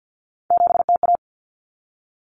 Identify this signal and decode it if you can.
Morse code, heard as a keyed tone, about 50 wpm, 700 Hz. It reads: 7TA
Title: Common Cut Numbers - 3-Digit 50wpm